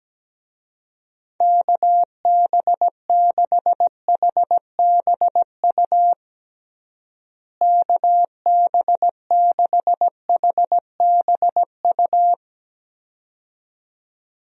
Morse code, heard as a keyed tone, about 17 wpm, 700 Hz. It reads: KB6HBU KB6HBU
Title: USA Call Signs - 6 Characters - Repeated 2x - 17wpm